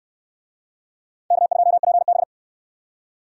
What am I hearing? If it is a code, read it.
Morse code, heard as a keyed tone, about 63 wpm, 700 Hz. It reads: K2PB